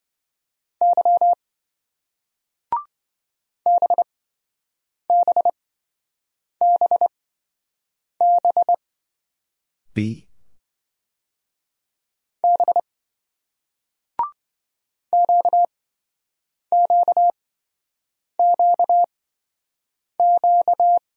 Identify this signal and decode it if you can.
Morse code, heard as a keyed tone, about 28 wpm, spaced as for 7 wpm, 700 Hz. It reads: Y BBBB B QQQQ